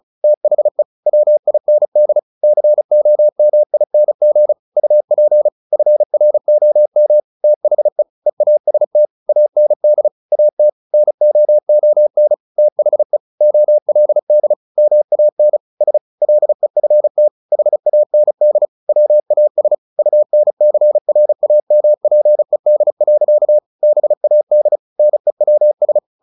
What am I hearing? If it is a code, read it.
Morse code, heard as a keyed tone, about 34 wpm, 600 Hz. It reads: THE WIND COMING UP FROM THE EAST AND AT NOON THE OLD MAN S LEFT HAND WAS UNCRAMPED. BAD NEWS
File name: the old man and the sea / the old man and the sea 1 35wpm